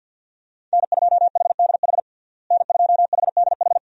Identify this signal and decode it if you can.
Morse code, heard as a keyed tone, about 52 wpm, 700 Hz. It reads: N2HDH N2HDH
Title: USA Call Signs - 5 Characters - Repeated 2x - 50wpm